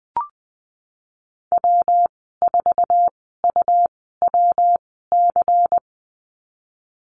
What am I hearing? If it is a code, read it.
Morse code, heard as a keyed tone, about 20 wpm, spaced as for 15 wpm, 700 Hz. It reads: W4UWC
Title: 122 Lesson 20g - Number 3 - Call Signs 20wpm